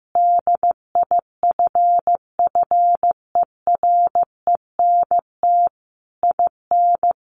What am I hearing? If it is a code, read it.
Morse code, heard as a keyed tone, about 14 wpm, 700 Hz. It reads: DIFFERENT IN
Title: Sentences from Top 1400 Words 15wpm